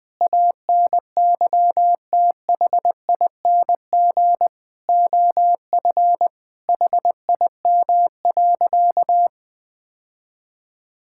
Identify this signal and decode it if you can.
Morse code, heard as a keyed tone, about 20 wpm, 700 Hz. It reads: ANYTHING OF HIM.
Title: Sentences from Top 1600 Words 20wpm